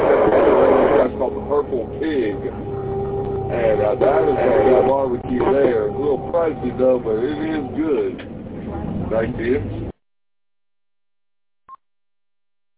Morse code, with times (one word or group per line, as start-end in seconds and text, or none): none